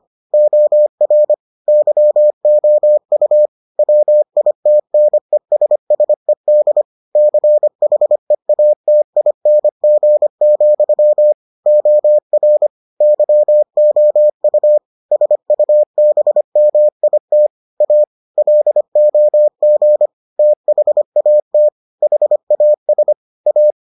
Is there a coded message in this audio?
Yes: Morse code from 0.3 to 23.7 s, heard as a keyed tone, about 25 wpm, 600 Hz. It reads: OR YOU WITNESSED CHEATING, OR YOU SUBMIT A LOG THAT HAS A